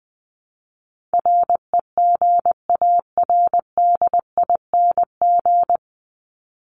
Morse code, heard as a keyed tone, about 20 wpm, 700 Hz. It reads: REGARDING
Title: ING Suffix - Top 100 Words with Base Words 20wpm